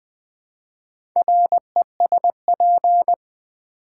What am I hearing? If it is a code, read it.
Morse code, heard as a keyed tone, about 20 wpm, 700 Hz. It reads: RESP
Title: Build A Word - 4 Letters At A Time - 12-Letter Words - 20wpm